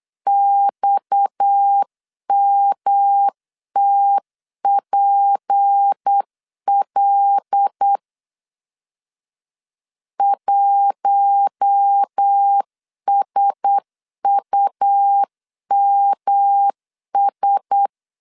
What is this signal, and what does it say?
Morse code, heard as a keyed tone, about 8 wpm, 800 Hz. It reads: XMTPL 1SUMS
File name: CW / DET / 1 6 1500C S C 8wpm 5219